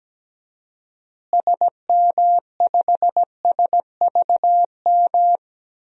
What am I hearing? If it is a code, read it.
Morse code, heard as a keyed tone, about 17 wpm, 700 Hz. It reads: SM5SVM
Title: DXCC-100 Call Signs - 6 Characters - Repeated 2x - 17wpm